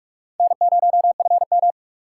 Morse code, heard as a keyed tone, about 56 wpm, 700 Hz. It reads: N0FM
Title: Speed Racing - USA 1x2 Call Signs 45wpm